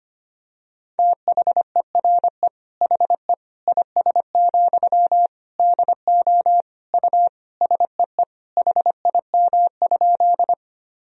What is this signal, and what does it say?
Morse code, heard as a keyed tone, about 25 wpm, 700 Hz. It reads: THERE HE IS, DO U SEE HIM?
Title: Sentences from Top 100 Words and Abbreviations 25wpm